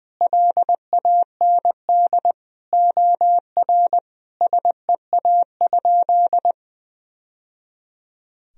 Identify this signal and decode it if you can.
Morse code, heard as a keyed tone, about 20 wpm, 700 Hz. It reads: LAND OR SEA?
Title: Sentences from Top 1000 Words - Review Part 1 - 20wpm